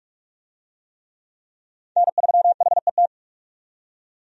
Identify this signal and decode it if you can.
Morse code, heard as a keyed tone, about 46 wpm, 700 Hz. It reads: N3HET
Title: USA Call Signs - 5 Characters - 45wpm